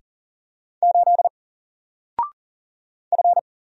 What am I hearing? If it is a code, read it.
Morse code, heard as a keyed tone, about 40 wpm, 700 Hz. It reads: 8 F
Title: Single Character (Letter:Number:Punctuation:Prosign) 40wpm